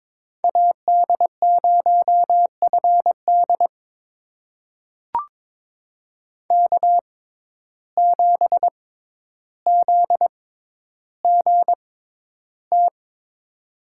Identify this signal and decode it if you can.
Morse code, heard as a keyed tone, about 22 wpm, spaced as for 11 wpm, 700 Hz. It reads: AD0FD K7ZGT
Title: USA Call Signs - 5 Characters - 6x Character Spacing - 22wpm